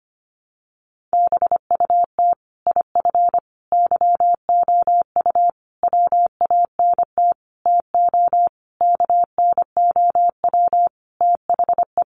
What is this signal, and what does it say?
Morse code, heard as a keyed tone, about 25 wpm, 700 Hz. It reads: BUT IF YOU WANT TO KNOW THE
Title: Sentences from Top 800 Words 25wpm